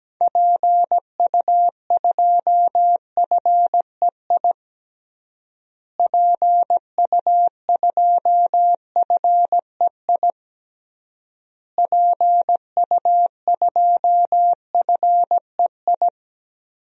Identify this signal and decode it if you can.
Morse code, heard as a keyed tone, about 17 wpm, 700 Hz. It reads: PU2FEI PU2FEI PU2FEI
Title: DXCC-100 Call Signs - 6 Characters - Repeated 3x - 17wpm